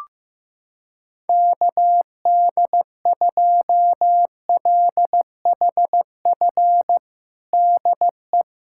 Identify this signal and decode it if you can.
Morse code, heard as a keyed tone, about 15 wpm, 700 Hz. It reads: KD2LHF DE
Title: USA Callsign DE Callsign - 6 Characters - 15wpm